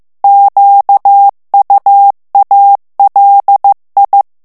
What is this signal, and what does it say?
Morse code, heard as a keyed tone, about 15 wpm, 800 Hz. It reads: QUALI
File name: morse mp3 / ITALIEN texte clair 2 vitesse 14 mots minute